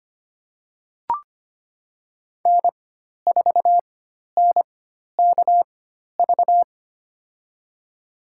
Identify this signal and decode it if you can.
Morse code, heard as a keyed tone, about 25 wpm, spaced as for 12 wpm, 700 Hz. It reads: N4NKV
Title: USA Call Signs - 5 Characters - 4x Character Spacing - 25wpm